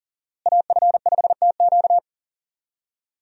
Morse code, heard as a keyed tone, about 40 wpm, 700 Hz. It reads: AF5TQ
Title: USA Call Signs - 5 Characters - Repeated 2x - 40wpm